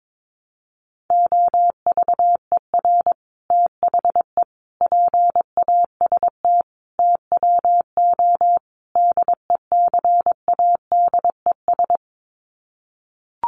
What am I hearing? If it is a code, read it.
Morse code, heard as a keyed tone, about 22 wpm, 700 Hz. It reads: OVER THE PAST TWO DECADES
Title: Sets of 5 Words Encore2 22wpm